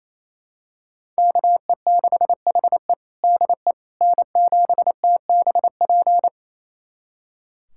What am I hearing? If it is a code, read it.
Morse code, heard as a keyed tone, about 27 wpm, 700 Hz. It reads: KE6HE DE N7TBP
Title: USA Callsign DE Callsign - 5 Characters - 28wpm